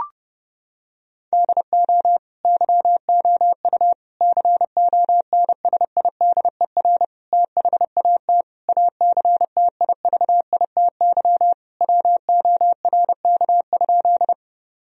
Morse code, heard as a keyed tone, about 30 wpm, 700 Hz. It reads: DO YOU CONSIDER THAT ACTIVITY WORK?
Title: Sentences from Top 1000 Words - Review Part 4 - 30wpm